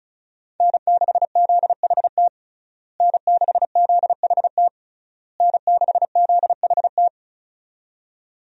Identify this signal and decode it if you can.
Morse code, heard as a keyed tone, about 36 wpm, 700 Hz. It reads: N6ZHT N6ZHT N6ZHT
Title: USA Call Signs - 5 Characters - Repeated 3x - 35wpm